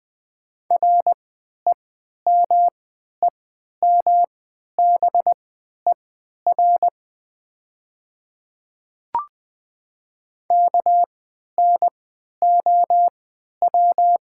Morse code, heard as a keyed tone, about 20 wpm, spaced as for 11 wpm, 700 Hz. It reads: REMEMBER KNOW
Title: 3x Character Spacing - Top 500 Words - 20wpm